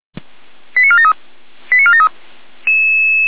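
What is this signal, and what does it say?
Five-tone selective call (ZVEI), heard as quick runs of five tones, tones of about 70 ms. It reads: 89473 89473
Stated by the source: ZVEI five-tone selective call received on an SDR.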